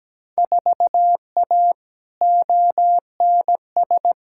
Morse code, heard as a keyed tone, about 17 wpm, 700 Hz. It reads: 4A ONS
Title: Field Day Exchanges - 17wpm